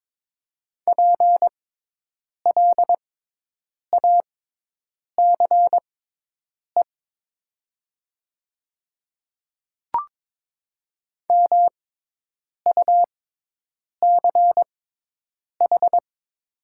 Morse code, heard as a keyed tone, about 22 wpm, spaced as for 8 wpm, 700 Hz. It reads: PLACE MUCH
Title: 6x Character Spacing - Top 200 Words - 22wpm